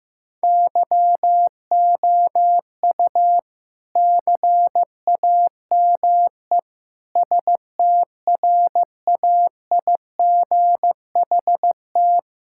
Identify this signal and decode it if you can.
Morse code, heard as a keyed tone, about 15 wpm, 700 Hz. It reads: YOU CAME STRAIGHT
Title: Sentences from Top 1000 Words - Review Part 6 - 15wpm